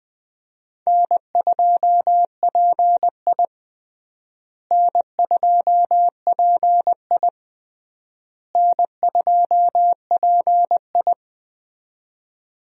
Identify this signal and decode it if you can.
Morse code, heard as a keyed tone, about 19 wpm, 700 Hz. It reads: N2PI N2PI N2PI
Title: USA Call Signs - 4 Characters - Repeated 3x - 20wpm